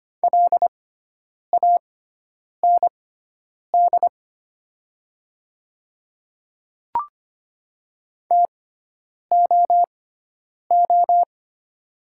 Morse code, heard as a keyed tone, about 25 wpm, spaced as for 9 wpm, 700 Hz. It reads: LAND TOO
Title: 6x Character Spacing - Top 200 Words - 25wpm